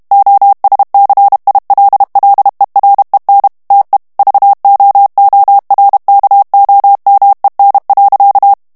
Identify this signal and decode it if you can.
Morse code, heard as a keyed tone, about 32 wpm, 800 Hz. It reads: OSCILLEREN TE VOORKOMEN.
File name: HOLLANDAIS texte clair 2 vitesse 30 mots minute